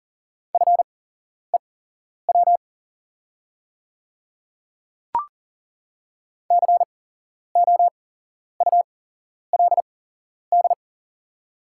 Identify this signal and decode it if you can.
Morse code, heard as a keyed tone, about 40 wpm, spaced as for 11 wpm, 690 Hz. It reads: FEW COULD
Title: 8x Character Spacing - Top 300 Words - 40wpm